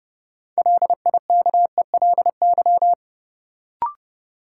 Morse code, heard as a keyed tone, about 30 wpm, 700 Hz. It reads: LIKELY
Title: Sentences from Top 600 Words - New Word Drill - 30wpm